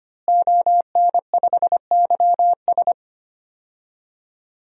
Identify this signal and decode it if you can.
Morse code, heard as a keyed tone, about 25 wpm, 700 Hz. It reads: ON5YS
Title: DXCC-100 Call Signs - 5 Characters - Repeated 3x - 25wpm